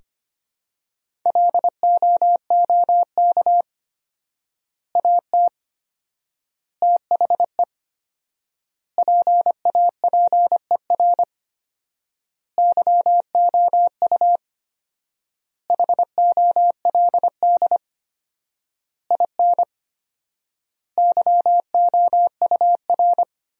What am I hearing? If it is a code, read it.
Morse code, heard as a keyed tone, about 25 wpm, 700 Hz. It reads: LOOK AT THE PAPER YOU HOLD IN YOUR
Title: 4x Word-Spacing - Sentences from Top 400 Words - 25wpm